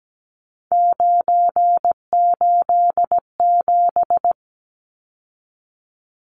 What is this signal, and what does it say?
Morse code, heard as a keyed tone, about 17 wpm, 700 Hz. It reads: 987
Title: Numbers - 3-Digit 17wpm